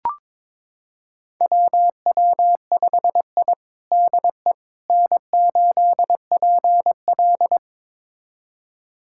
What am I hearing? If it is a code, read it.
Morse code, heard as a keyed tone, about 22 wpm, 700 Hz. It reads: WW5I DE N8PL
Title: USA Callsign DE Callsign - 4 Characters - 22wpm